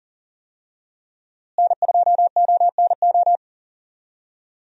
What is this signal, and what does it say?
Morse code, heard as a keyed tone, about 40 wpm, 700 Hz. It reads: N2ONO
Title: USA Call Signs - 5 Characters - Repeated 3x - 40wpm